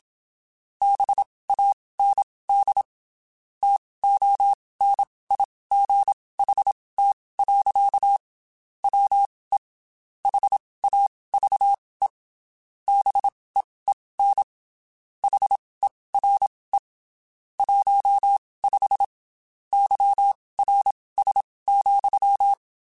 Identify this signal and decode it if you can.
Morse code, heard as a keyed tone, about 27 wpm, spaced as for 19 wpm, 800 Hz. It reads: BAND TONIGHT. WE HAVE BEEN HERE 15 YRS,
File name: QSOs / QSOs-mp3s / WD0FBT 1@25